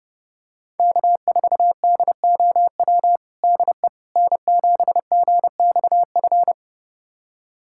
Morse code, heard as a keyed tone, about 30 wpm, 700 Hz. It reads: K4DOW DE N7GXF
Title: USA Callsign DE Callsign - 5 Characters - 30wpm